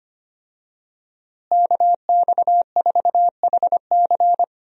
Morse code, heard as a keyed tone, about 25 wpm, 700 Hz. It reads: KX4HC
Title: USA Call Signs - 5 Characters - Repeated 2x - 25wpm